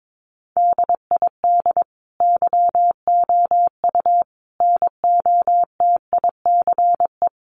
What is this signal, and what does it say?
Morse code, heard as a keyed tone, about 22 wpm, 700 Hz. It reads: DID YOU NOTICE
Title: Sentences from Top 1000 Words 22wpm